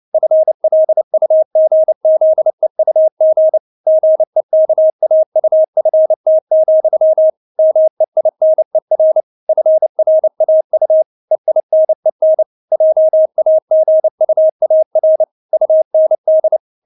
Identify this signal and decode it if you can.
Morse code, heard as a keyed tone, about 29 wpm, 630 Hz. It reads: FLUGZEUG GEKAUFT, MEINER FRAU EINEN JAGUAR UND
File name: Witz10 29wpm